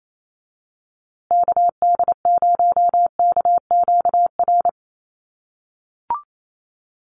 Morse code, heard as a keyed tone, about 28 wpm, 700 Hz. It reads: KD0KQR
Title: USA Call Signs Repeated 2x 28wpm